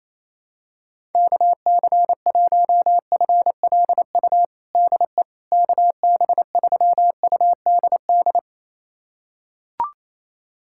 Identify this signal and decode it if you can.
Morse code, heard as a keyed tone, about 28 wpm, 700 Hz. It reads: KC1FLU DE KB3UDD
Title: USA Callsign DE Callsign - 6 Characters - 28wpm